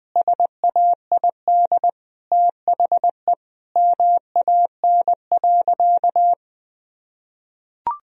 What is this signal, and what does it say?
Morse code, heard as a keyed tone, about 21 wpm, 700 Hz. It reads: SAID THE MAN.
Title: Sentences from Top 1000 Words - Review Part 3 - 20wpm